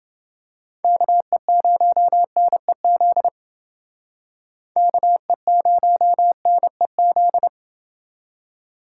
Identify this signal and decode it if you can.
Morse code, heard as a keyed tone, about 28 wpm, 700 Hz. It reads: KE0NEZ KE0NEZ